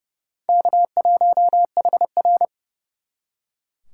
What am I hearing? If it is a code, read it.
Morse code, heard as a keyed tone, about 34 wpm, 700 Hz. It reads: K1HR
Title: Speed Racing - USA 1x2 Call Signs 30wpm